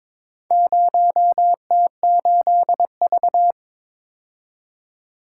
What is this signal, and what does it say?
Morse code, heard as a keyed tone, about 22 wpm, 700 Hz. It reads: 0T8V
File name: Four Letter-Number Combinations - 22wpm